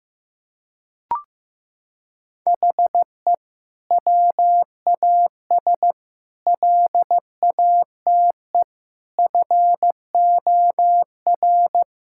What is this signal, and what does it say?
Morse code, heard as a keyed tone, about 15 wpm, 700 Hz. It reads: HE WAS LATE FOR
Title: Sentences from Top 1600 Words 15wpm